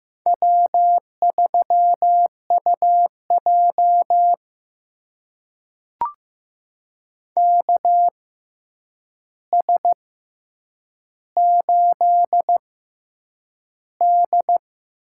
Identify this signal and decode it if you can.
Morse code, heard as a keyed tone, about 15 wpm, spaced as for 8 wpm, 700 Hz. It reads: W3UJ KS8D
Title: USA Call Signs - 4 Characters - 6x Character Spacing - 15wpm